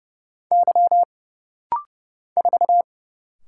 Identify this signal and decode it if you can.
Morse code, heard as a keyed tone, about 29 wpm, 700 Hz. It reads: Y 4
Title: Single Letter-Number Alt - Rapid-Fire 30wpm